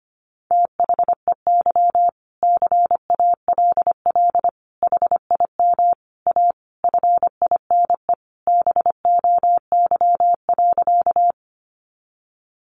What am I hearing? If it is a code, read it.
Morse code, heard as a keyed tone, about 25 wpm, 700 Hz. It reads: THEY CALL HIM A FINE BOY.
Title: Sentences from Top 500 Words 25wpm